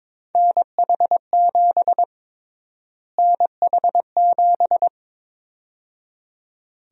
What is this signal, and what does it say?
Morse code, heard as a keyed tone, about 21 wpm, 700 Hz. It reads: NH7 NH7
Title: USA Call Sign Prefixes - Repeated 2x - 22wpm